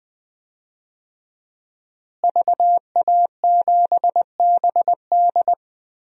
Morse code, heard as a keyed tone, about 20 wpm, 700 Hz. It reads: VA7BD
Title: DXCC-100 Call Signs - 5 Characters - Repeated 2x - 20wpm